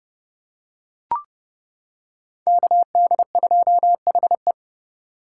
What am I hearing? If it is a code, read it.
Morse code, heard as a keyed tone, about 30 wpm, 700 Hz. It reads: KD2HE
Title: K1USN SST QSO Practice - 30wpm